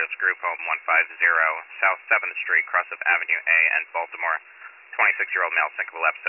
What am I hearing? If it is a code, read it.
Emergency Alert System (SAME) tones (header bursts), heard as harsh buzzing bursts, with speech. It is heard from 1.0 to 5.3 s.